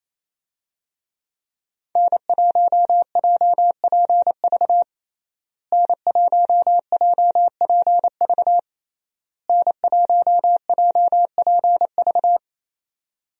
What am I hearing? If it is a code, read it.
Morse code, heard as a keyed tone, about 28 wpm, 700 Hz. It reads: N1JPV N1JPV N1JPV
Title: USA Call Signs - 5 Characters - Repeated 3x - 28wpm